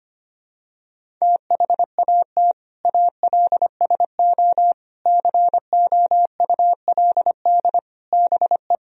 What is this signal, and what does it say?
Morse code, heard as a keyed tone, about 25 wpm, 700 Hz. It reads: THAT ALSO COULD BE
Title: Sentences from Top 1000 Words - Review Part 1 - 25wpm